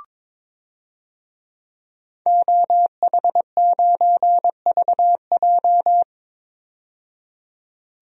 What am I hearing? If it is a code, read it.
Morse code, heard as a keyed tone, about 22 wpm, 700 Hz. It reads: OH9VJ
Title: DXCC-100 Call Signs - 5 Characters - 22wpm